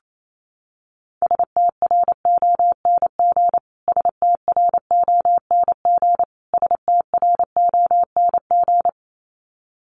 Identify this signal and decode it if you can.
Morse code, heard as a keyed tone, about 27 wpm, 700 Hz. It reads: STRONG STRONG STRONG
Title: Top 500 Words Repeated 28wpm